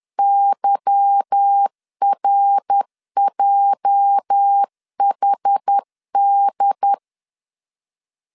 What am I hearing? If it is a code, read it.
Morse code, heard as a keyed tone, about 10 wpm, 800 Hz. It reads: YRJHD
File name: LEZIONI / 8 YJ45 / 8 YJ 10wpm 2651